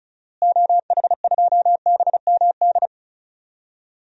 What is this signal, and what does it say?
Morse code, heard as a keyed tone, about 36 wpm, 700 Hz. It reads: OH2BMD
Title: DXCC-100 Call Signs - 6 Characters - Repeated 3x - 35wpm